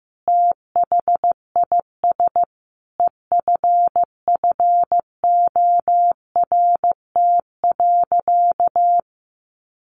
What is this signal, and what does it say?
Morse code, heard as a keyed tone, about 15 wpm, 700 Hz. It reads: THIS EFFORT.
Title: Sentences from Top 1400 Words 15wpm